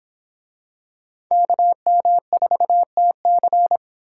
Morse code, heard as a keyed tone, about 23 wpm, 700 Hz. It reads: KM4TC